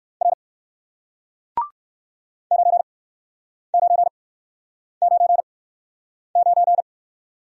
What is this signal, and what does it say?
Morse code, heard as a keyed tone, about 54 wpm, spaced as for 9 wpm, 700 Hz. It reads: R 9999
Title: Speed-Racing Single Letter-Number 45wpm